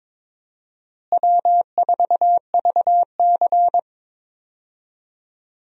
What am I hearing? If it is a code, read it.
Morse code, heard as a keyed tone, about 22 wpm, 700 Hz. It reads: W4VC
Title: USA Call Signs - 4 Characters - Repeated 3x - 22wpm